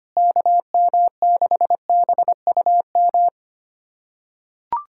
Morse code, heard as a keyed tone, about 25 wpm, 700 Hz. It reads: KM6BUM
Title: USA Callsign DE Callsign 25wpm